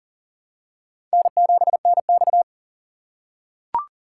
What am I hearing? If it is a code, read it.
Morse code, heard as a keyed tone, about 37 wpm, 690 Hz. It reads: N7NX
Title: USA Call Signs - 6x Character Spacing - 40wpm